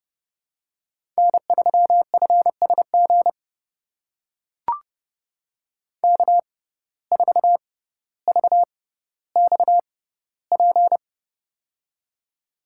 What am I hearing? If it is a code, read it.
Morse code, heard as a keyed tone, about 29 wpm, spaced as for 15 wpm, 700 Hz. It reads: N3FSG K4VXP
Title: USA Call Signs - 5 Characters - 6x Character Spacing - 30wpm